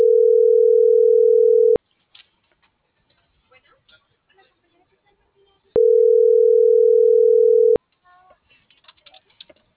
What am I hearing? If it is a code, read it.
Ringback tone (North America), 440 + 480 Hz.